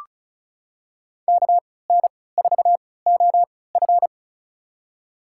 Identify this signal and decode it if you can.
Morse code, heard as a keyed tone, about 36 wpm, spaced as for 20 wpm, 700 Hz. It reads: KN4OF
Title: USA Call Signs - 3x Character Spacing - 35wpm